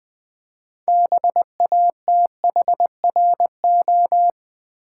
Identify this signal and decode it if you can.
Morse code, heard as a keyed tone, about 21 wpm, 700 Hz. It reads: BATHRO
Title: Build A Word - 2 Letters At A Time - 8-Letter Words - 20wpm